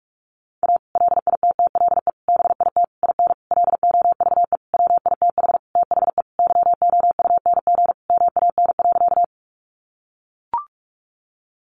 Morse code, heard as a keyed tone, about 44 wpm, 700 Hz. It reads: A LITTLE BIT IN LOVE WITH THE YOUNG MAN.
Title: Sentences from Top 900 Words 45wpm